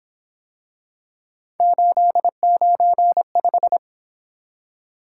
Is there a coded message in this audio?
Yes: Morse code from 1.6 to 3.8 s, heard as a keyed tone, about 29 wpm, 700 Hz. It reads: 895